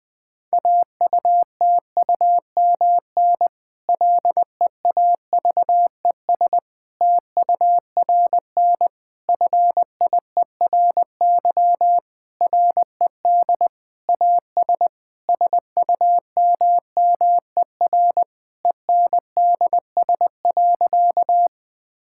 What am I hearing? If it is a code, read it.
Morse code, heard as a keyed tone, about 20 wpm, 700 Hz. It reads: AUTUMN LEAVES TURN FIERY RED AS SUMMER ENDS.
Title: Difficult Sentences with Hints - 20wpm